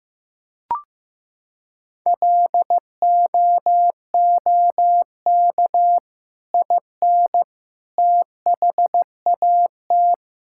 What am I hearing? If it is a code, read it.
Morse code, heard as a keyed tone, about 15 wpm, 700 Hz. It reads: LOOK IN THAT
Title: Sentences from Top 1000 Words - Review Part 5 - 15wpm